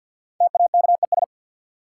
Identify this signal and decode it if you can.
Morse code, heard as a keyed tone, about 51 wpm, 700 Hz. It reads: TAKES
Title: Sets of 5 Words Encore - New Word Drill - 50wpm